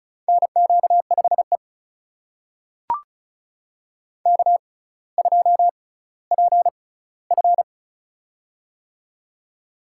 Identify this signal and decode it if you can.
Morse code, heard as a keyed tone, about 36 wpm, spaced as for 18 wpm, 700 Hz. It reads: NQ5E K2PF
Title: USA Call Signs - 4 Characters - 6x Character Spacing - 35wpm